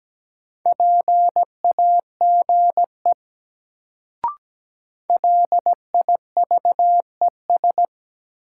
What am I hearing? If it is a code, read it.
Morse code, heard as a keyed tone, about 17 wpm, 700 Hz. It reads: PAGE LIVES
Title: Sets of 5 Words Encore - New Word Drill - 17wpm